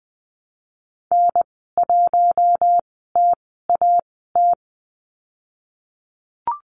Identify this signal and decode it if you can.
Morse code, heard as a keyed tone, about 20 wpm, spaced as for 15 wpm, 700 Hz. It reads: N1TAT
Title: 055 Lesson 10g - Letter r - Call Signs 20wpm